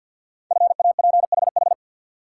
Morse code, heard as a keyed tone, about 50 wpm, 700 Hz. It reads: FAPHH
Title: Code Groups - 5 Letters - 50wpm